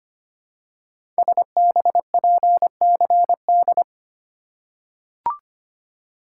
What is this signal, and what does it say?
Morse code, heard as a keyed tone, about 25 wpm, 700 Hz. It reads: SBPCD
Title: Code Groups - 5 Letters - 25wpm